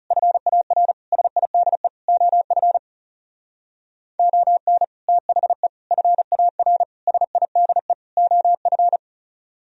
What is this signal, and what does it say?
Morse code, heard as a keyed tone, about 36 wpm, 700 Hz. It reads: FAR SIDE OF ON THE FAR SIDE OF
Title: Speed-Racing - Sets of 5 Words 30wpm